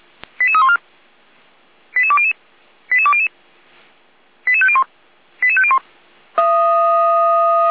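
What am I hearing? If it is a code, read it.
Five-tone selective call (ZVEI), heard as quick runs of five tones, tones of about 70 ms. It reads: 88314 88200 88200 88501 88501
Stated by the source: ZVEI five-tone call received by SDR.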